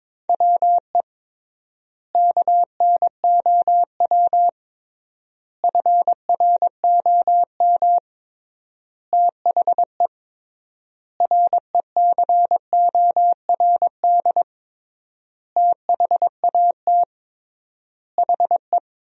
Morse code, heard as a keyed tone, about 22 wpm, 700 Hz. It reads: WE KNOW FROM THE RECORD THAT HE
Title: 3x Word-Spacing - Sentences from Top 500 Words - 22wpm